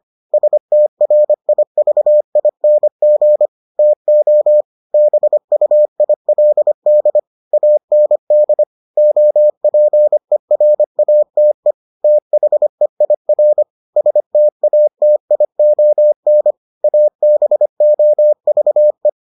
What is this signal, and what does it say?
Morse code, heard as a keyed tone, about 25 wpm, 600 Hz. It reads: STRIVING TO BUILD AND OPERATE THEIR STATION ABOVE